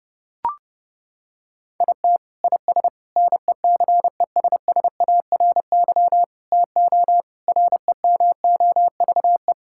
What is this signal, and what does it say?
Morse code, heard as a keyed tone, about 30 wpm, 700 Hz. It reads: IT IS NECESSARY TO REMOVE
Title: Sentences from Top 1000 Words - Review Part 5 - 30wpm